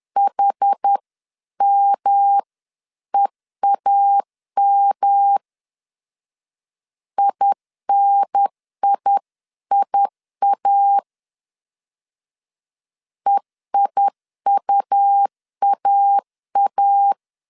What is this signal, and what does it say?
Morse code, heard as a keyed tone, about 10 wpm, 800 Hz. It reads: HMEAM INIIA EIUAA
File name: 10wpm